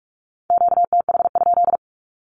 Morse code, heard as a keyed tone, about 46 wpm, 700 Hz. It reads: QTH?
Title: Q-Codes with Spoken Meaning - 45wpm